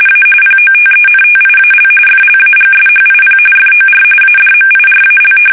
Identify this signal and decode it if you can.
RTTY, about 45.45 baud. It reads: A POTENTIAL FLAGSHIP FOR ADMIRAL
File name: fsk850 / battleshipnj